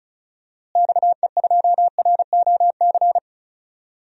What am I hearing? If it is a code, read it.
Morse code, heard as a keyed tone, about 36 wpm, 700 Hz. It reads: XE2ROC
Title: DXCC-100 Call Signs - 6 Characters - Repeated 2x - 35wpm